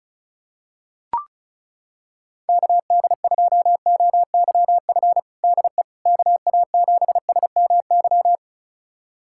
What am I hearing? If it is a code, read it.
Morse code, heard as a keyed tone, about 34 wpm, 700 Hz. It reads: KD2OYF DE KA7SMY
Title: USA Callsign DE Callsign - 6 Characters - 35wpm